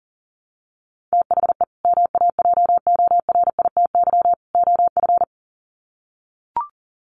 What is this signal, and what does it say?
Morse code, heard as a keyed tone, about 40 wpm, 700 Hz. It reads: THE MAJORITY OF
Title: Sets of 3 Words 40wpm